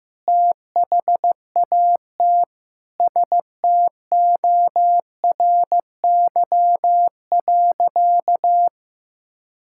Morse code, heard as a keyed tone, about 15 wpm, 700 Hz. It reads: THAT STORY.
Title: Sentences from Top 1000 Words - Review Part 5 - 15wpm